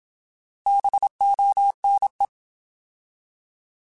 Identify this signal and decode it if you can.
Morse code, heard as a keyed tone, about 27 wpm, 800 Hz. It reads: BONE
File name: Alphabetical / Alphabetical mp3s / Bs @25